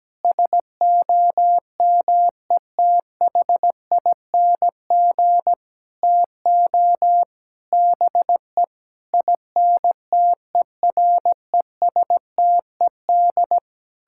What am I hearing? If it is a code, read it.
Morse code, heard as a keyed tone, about 17 wpm, 700 Hz. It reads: SOMETHING TO BE INTERESTED
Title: Sentences from Top 1000 Words - Review Part 6 - 17wpm